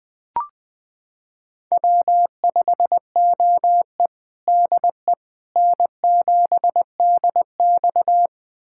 Morse code, heard as a keyed tone, about 20 wpm, 700 Hz. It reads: W5OE DE N7DX
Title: USA Callsign DE Callsign - 4 Characters - 20wpm